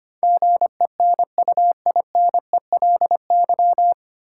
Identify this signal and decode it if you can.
Morse code, heard as a keyed tone, about 25 wpm, 700 Hz. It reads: GENUINELY
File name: Nine Letter Words - 25wpm